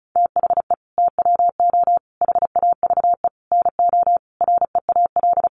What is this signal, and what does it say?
Morse code, heard as a keyed tone, about 34 wpm, 700 Hz. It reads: THE TWO HAVE NO REAL
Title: Sentences from Top 1300 Words 35wpm